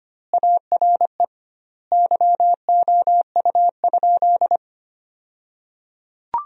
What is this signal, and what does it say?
Morse code, heard as a keyed tone, about 25 wpm, 700 Hz. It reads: ARE YOU?
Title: 2x Word-Spacing - Sentences from Top 100 Words - 25wpm